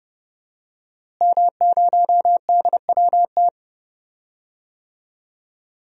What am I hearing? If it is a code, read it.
Morse code, heard as a keyed tone, about 30 wpm, 700 Hz. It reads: M0DWT
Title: DXCC-100 Call Signs - 5 Characters - Repeated 3x - 30wpm